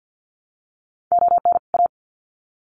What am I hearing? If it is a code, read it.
Morse code, heard as a keyed tone, about 50 wpm, 700 Hz. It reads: ON A
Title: Sets of 2 Words 50wpm